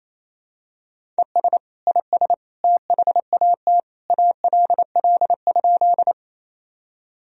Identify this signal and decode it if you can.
Morse code, heard as a keyed tone, about 28 wpm, 700 Hz. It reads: ES IS THAT ALL?
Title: Sentences from Top 100 Words and Abbreviations 28wpm